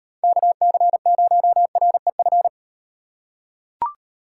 Morse code, heard as a keyed tone, about 38 wpm, 700 Hz. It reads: KC0REF